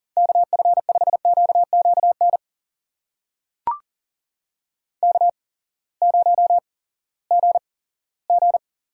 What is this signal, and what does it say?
Morse code, heard as a keyed tone, about 42 wpm, spaced as for 20 wpm, 700 Hz. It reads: KF5QQN K0GG
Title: USA Call Signs - 8x Character Spacing - 40wpm